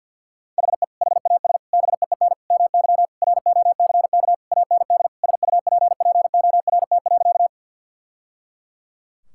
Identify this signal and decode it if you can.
Morse code, heard as a keyed tone, about 46 wpm, 700 Hz. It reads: HE HAS BEEN MY ROCK AND SUPPORT.
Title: Sentences from Top 1000 Words - Review Part 5 - 50wpm